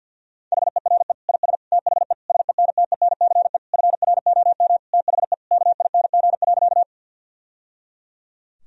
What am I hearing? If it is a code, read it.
Morse code, heard as a keyed tone, about 51 wpm, 700 Hz. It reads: HERE IS THE SENTENCE FROM THE KING.
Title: Sentences from Top 1000 Words - Review Part 1 - 50wpm